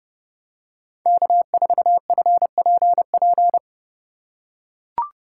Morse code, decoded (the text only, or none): K4FPP